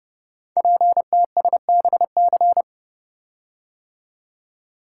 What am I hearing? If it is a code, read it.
Morse code, heard as a keyed tone, about 28 wpm, 700 Hz. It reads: PTSBC
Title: Code Groups - 5 Letters - 30wpm